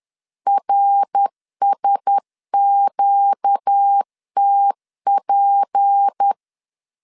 Morse code, heard as a keyed tone, about 10 wpm, 800 Hz. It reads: RSQTP